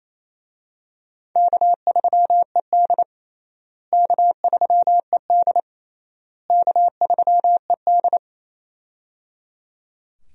Morse code, heard as a keyed tone, about 27 wpm, 700 Hz. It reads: K3ED K3ED K3ED
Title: USA Call Signs - 4 Characters - Repeated 3x - 28wpm